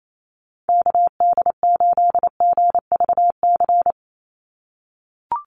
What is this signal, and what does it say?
Morse code, heard as a keyed tone, about 28 wpm, 700 Hz. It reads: KD8GVC
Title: USA Call Signs 28wpm